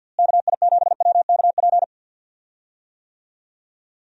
Morse code, heard as a keyed tone, about 51 wpm, 700 Hz. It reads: KI7WKP
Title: USA Call Signs - 6 Characters - Repeated 2x - 50wpm